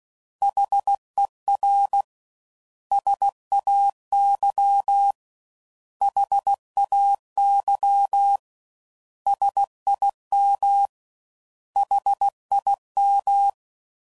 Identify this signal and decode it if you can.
Morse code, heard as a keyed tone, about 16 wpm, 800 Hz. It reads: HER SAY HAY SIM HIM
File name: Prefixes / By First Letter / S vs H